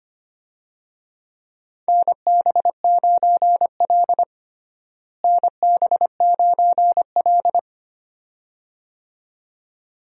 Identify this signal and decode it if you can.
Morse code, heard as a keyed tone, about 25 wpm, 700 Hz. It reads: NB9L NB9L
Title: USA Call Signs - 4 Characters - Repeated 2x - 25wpm